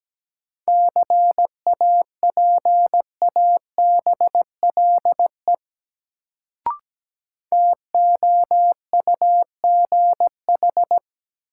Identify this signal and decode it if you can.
Morse code, heard as a keyed tone, about 17 wpm, 700 Hz. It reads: CAPABLE TOUGH
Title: Sentences from Top 1200 Words - New Word Drill Alt - 17wpm